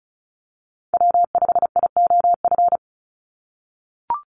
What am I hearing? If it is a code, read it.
Morse code, heard as a keyed tone, about 36 wpm, 700 Hz. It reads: W5IOF
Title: USA Call Signs 35wpm